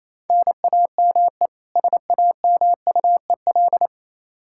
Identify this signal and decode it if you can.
Morse code, heard as a keyed tone, about 28 wpm, 700 Hz. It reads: NAME SAMUEL
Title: Practice QSO - Exchange 1 - Beginning - 28wpm